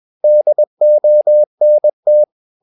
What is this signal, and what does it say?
Morse code, heard as a keyed tone, about 21 wpm, 600 Hz. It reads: DONT
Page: CW med Gnister